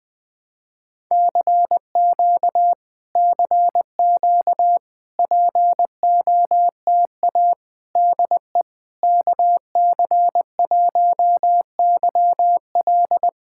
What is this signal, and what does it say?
Morse code, heard as a keyed tone, about 20 wpm, 700 Hz. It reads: CQ CQ POTA DE KC1YL
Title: POTA QSO Practice - 20wpm